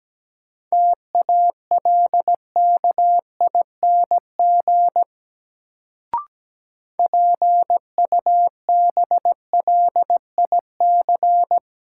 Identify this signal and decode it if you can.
Morse code, heard as a keyed tone, about 17 wpm, 700 Hz. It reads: TALKING PUBLIC
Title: Sets of 2 Words - New Word Drill - 17wpm